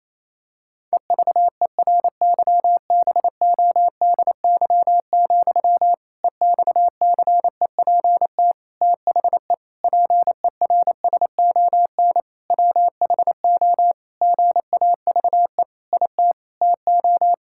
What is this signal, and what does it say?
Morse code, heard as a keyed tone, about 28 wpm, 700 Hz. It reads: EVERYBODY, EXCEPT THE PERSON WHO GAVE IT TO
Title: Sentences from Top 1600 Words 28wpm